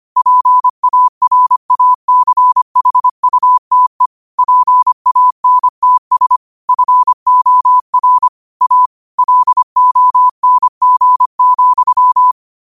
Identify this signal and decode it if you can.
Morse code, heard as a keyed tone, about 25 wpm, 1000 Hz. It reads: PARACHUTE PANTS FOR A LONG,